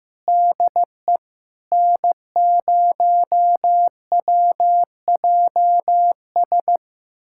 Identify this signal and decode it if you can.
Morse code, heard as a keyed tone, about 15 wpm, 700 Hz. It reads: DE N0WJS
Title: USA Callsign DE Callsign - 5 Characters - 15wpm